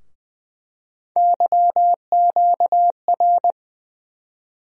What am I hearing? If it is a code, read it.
Morse code, heard as a keyed tone, about 20 wpm, 700 Hz. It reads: YQR
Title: USA Call Sign Postfixes - 3 Letters 20wpm